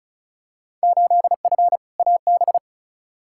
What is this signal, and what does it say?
Morse code, heard as a keyed tone, about 36 wpm, 700 Hz. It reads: 8F AB
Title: Field Day Exchanges Encore - 35wpm